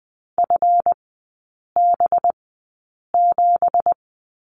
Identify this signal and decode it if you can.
Morse code, heard as a keyed tone, about 20 wpm, spaced as for 8 wpm, 690 Hz. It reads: FB7
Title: 205 Lesson 33e - Letter j - Three Characters 20wpm